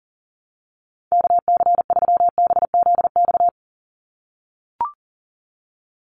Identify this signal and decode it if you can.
Morse code, heard as a keyed tone, about 40 wpm, 700 Hz. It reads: KC3BZX
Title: USA Call Signs 40wpm